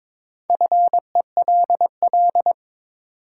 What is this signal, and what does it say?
Morse code, heard as a keyed tone, about 22 wpm, 700 Hz. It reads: FELL
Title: Sets of 5 Words Encore - New Word Drill - 22wpm